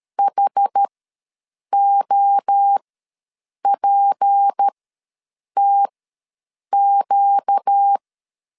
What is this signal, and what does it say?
Morse code, heard as a keyed tone, about 13 wpm, spaced as for 7 wpm, 800 Hz. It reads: HOPTQ
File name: LEZIONI / 10 SIGNS / 10B INT ws 12wpm 3527